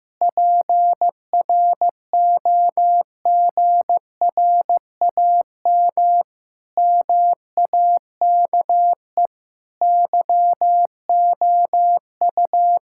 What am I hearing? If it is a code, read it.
Morse code, heard as a keyed tone, about 16 wpm, 700 Hz. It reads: PROGRAM MAKE YOU
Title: Sentences from Top 1000 Words - Review Part 5 - 15wpm